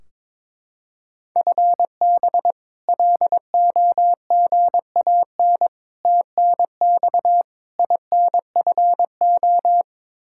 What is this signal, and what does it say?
Morse code, heard as a keyed tone, about 22 wpm, 700 Hz. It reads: FB LOGAN TNX INFO
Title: K1USN SST QSO Practice 22wpm